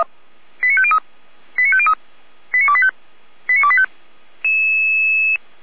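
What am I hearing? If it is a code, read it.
Five-tone selective call (ZVEI), heard as quick runs of five tones, tones of about 70 ms. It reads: 89503 89503 89285 89285
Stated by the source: ZVEI five-tone selective call 89280 received by SDR.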